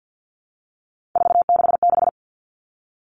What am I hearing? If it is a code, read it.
Morse code, heard as a keyed tone, about 50 wpm, 700 Hz. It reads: 466
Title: Numbers - 3-Digit 50wpm